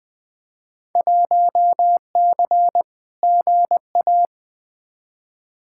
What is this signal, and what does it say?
Morse code, heard as a keyed tone, about 20 wpm, 700 Hz. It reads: 1C GA
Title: Field Day Exchanges Encore - 20wpm